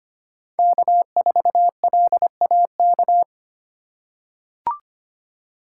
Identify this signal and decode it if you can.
Morse code, heard as a keyed tone, about 25 wpm, 700 Hz. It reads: K4LAK
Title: USA Call Signs - 5 Characters - 8x Character Spacing - 25wpm